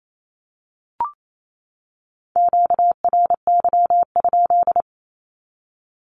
Morse code, heard as a keyed tone, about 28 wpm, 700 Hz. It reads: QRY?
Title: Q-Codes 28wpm